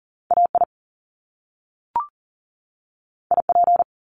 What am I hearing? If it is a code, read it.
Morse code, heard as a keyed tone, about 38 wpm, 700 Hz. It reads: AI IP
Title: Top Two Letter Combinations in Words 40wpm